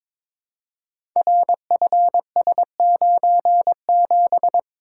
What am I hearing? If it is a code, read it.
Morse code, heard as a keyed tone, about 22 wpm, 700 Hz. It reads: RFS97
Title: Code Groups - 5 Letter:Numbers - 22wpm